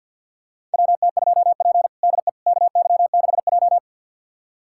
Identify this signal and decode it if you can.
Morse code, heard as a keyed tone, about 52 wpm, 700 Hz. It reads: WT2P DE KY6J
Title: USA Callsign DE Callsign - 4 Characters - 50wpm